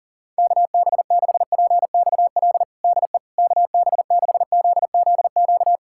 Morse code, heard as a keyed tone, about 40 wpm, 700 Hz. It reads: KB6PXL DE KB6ZZQ
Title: USA Callsign DE Callsign - 6 Characters - 40wpm